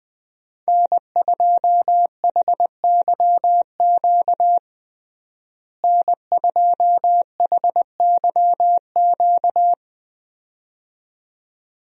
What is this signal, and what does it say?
Morse code, heard as a keyed tone, about 20 wpm, 700 Hz. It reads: N2HYQ N2HYQ
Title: USA Call Signs - 5 Characters - Repeated 2x - 20wpm